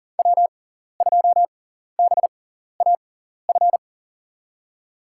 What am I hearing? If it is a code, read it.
Morse code, heard as a keyed tone, about 40 wpm, spaced as for 14 wpm, 700 Hz. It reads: W2BAF
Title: USA Call Signs - 5 Characters - 6x Character Spacing - 40wpm